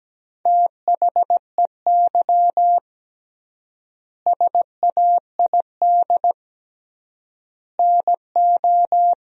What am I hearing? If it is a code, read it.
Morse code, heard as a keyed tone, about 17 wpm, 700 Hz. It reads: THEY SAID NO
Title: 3x Word-Spacing - Sentences from Top 100 Words - 17wpm